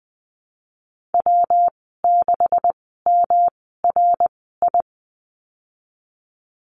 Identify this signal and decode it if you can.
Morse code, heard as a keyed tone, about 20 wpm, spaced as for 15 wpm, 700 Hz. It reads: W6MRI
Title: 154 Lesson 25g - Letter p - Call Signs 20wpm